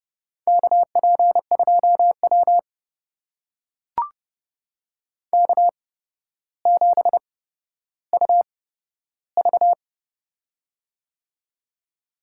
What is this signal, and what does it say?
Morse code, heard as a keyed tone, about 30 wpm, spaced as for 13 wpm, 700 Hz. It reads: KP2W K7UV
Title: USA Call Signs - 4 Characters - 8x Character Spacing - 30wpm